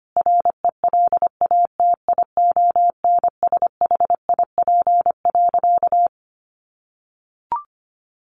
Morse code, heard as a keyed tone, about 25 wpm, 700 Hz. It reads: RELATIONSHIP.
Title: Sentences from Top 1100 Words 25wpm